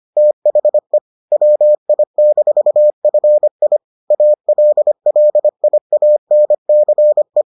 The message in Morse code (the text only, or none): THE WI-FI ALLIANCE